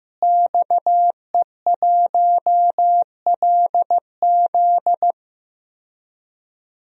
Morse code, heard as a keyed tone, about 15 wpm, 700 Hz. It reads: XE1LZ
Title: DXCC-100 Call Signs - 5 Characters - Repeated 2x - 15wpm